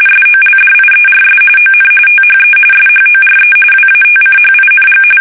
RTTY, about 45.45 baud. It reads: SOLDIERS ON THE SHORE WATCHING
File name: fsk850 / usnavy3